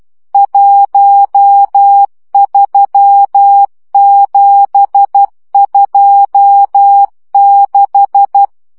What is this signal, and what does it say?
Morse code, heard as a keyed tone, about 12 wpm, 800 Hz. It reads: 13726
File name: morze vizsga 2011-2 10